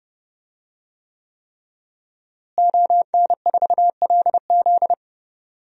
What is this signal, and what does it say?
Morse code, heard as a keyed tone, about 31 wpm, 700 Hz. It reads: ON4LZ
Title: DXCC-100 Call Signs - 5 Characters - Repeated 3x - 30wpm